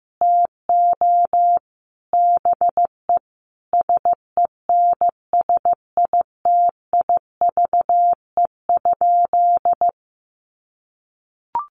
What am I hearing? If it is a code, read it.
Morse code, heard as a keyed tone, about 15 wpm, 700 Hz. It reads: TO BE SENSITIVE?
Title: Sentences from Top 1400 Words 15wpm